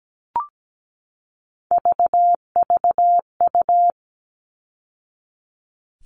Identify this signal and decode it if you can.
Morse code, heard as a keyed tone, about 17 wpm, 700 Hz. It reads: VVU
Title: All Cut Numbers - 3-Digit 17wpm